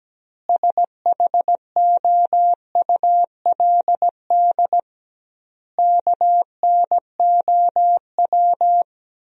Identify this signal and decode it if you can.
Morse code, heard as a keyed tone, about 17 wpm, 700 Hz. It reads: SHOULD KNOW
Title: 2x Word-Spacing - Sentences from Top 400 Words - 17wpm